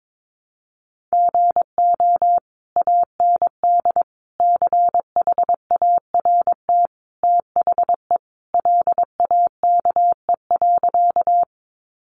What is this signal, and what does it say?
Morse code, heard as a keyed tone, about 22 wpm, 700 Hz. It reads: GO AND CHART THE LAKE.
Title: Sentences from Top 1300 Words 22wpm